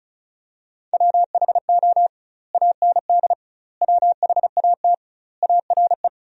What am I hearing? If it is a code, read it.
Morse code, heard as a keyed tone, about 34 wpm, 700 Hz. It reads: WHO AND WHAT ARE
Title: 2x Word-Spacing - Sentences from Top 100 Words - 35wpm